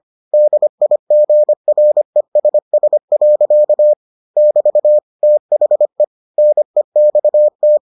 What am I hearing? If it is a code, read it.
Morse code, heard as a keyed tone, about 25 wpm, 600 Hz. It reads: DIGRESS. <BT> THE NEXT